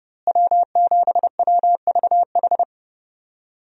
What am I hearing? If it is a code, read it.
Morse code, heard as a keyed tone, about 32 wpm, 700 Hz. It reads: W7WVH
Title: USA Call Signs - 5 Characters - Repeated 2x - 30wpm